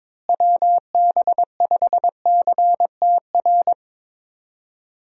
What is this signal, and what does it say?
Morse code, heard as a keyed tone, about 21 wpm, 700 Hz. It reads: WB5CTR
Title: USA Call Signs - 6 Characters - Repeated 2x - 22wpm